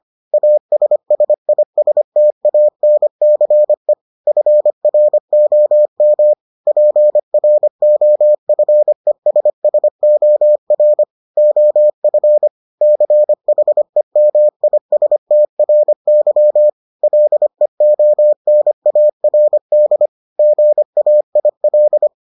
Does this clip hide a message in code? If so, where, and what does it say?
Yes: Morse code from 0.3 to 22.1 s, heard as a keyed tone, about 25 wpm, 600 Hz. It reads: ASSISTANCE FROM PROFESSOR OF CHEMISTRY LEONARD GAIL